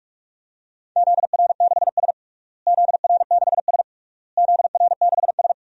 Morse code, heard as a keyed tone, about 46 wpm, 700 Hz. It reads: ZR6S ZR6S ZR6S
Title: DXCC-100 Call Signs - 4 Characters - Repeated 3x - 45wpm